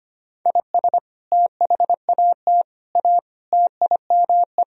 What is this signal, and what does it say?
Morse code, heard as a keyed tone, about 25 wpm, 700 Hz. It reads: IS THAT A TIME
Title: Sentences from Top 1000 Words - Review Part 2 - 25wpm